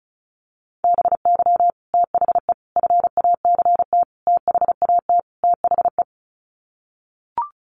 Morse code, heard as a keyed tone, about 34 wpm, 700 Hz. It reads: BY THE FACT THAT THE
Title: Sets of 5 Words Encore 35wpm